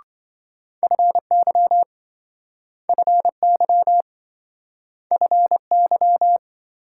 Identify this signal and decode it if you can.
Morse code, heard as a keyed tone, about 27 wpm, 700 Hz. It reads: FY FY FY